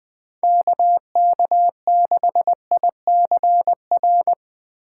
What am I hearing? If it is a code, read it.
Morse code, heard as a keyed tone, about 20 wpm, 700 Hz. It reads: KK6ICR
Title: Speed Racing - USA 2x3 Call Signs 17wpm